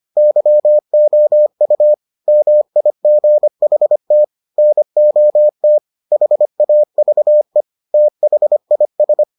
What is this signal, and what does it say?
Morse code, heard as a keyed tone, about 25 wpm, 600 Hz. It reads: YOU MIGHT NOT HAVE THIS